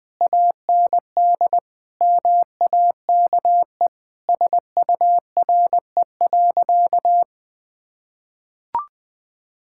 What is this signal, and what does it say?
Morse code, heard as a keyed tone, about 20 wpm, 700 Hz. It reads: AND MAKE SURE.
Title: Sentences from Top 1000 Words - Review Part 5 - 20wpm